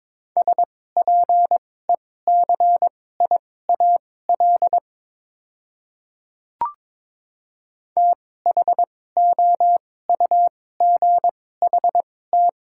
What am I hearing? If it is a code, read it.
Morse code, heard as a keyed tone, about 22 wpm, spaced as for 16 wpm, 690 Hz. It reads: SPECIAL THOUGHT
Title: 2x Character Spacing - Top 500 Words - 22wpm